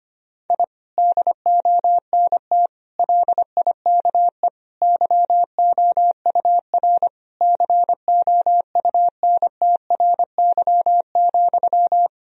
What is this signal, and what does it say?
Morse code, heard as a keyed tone, about 25 wpm, 700 Hz. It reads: I DONT LIKE YOUR COUNTRY,
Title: Sentences from Top 1000 Words - Review Part 3 - 25wpm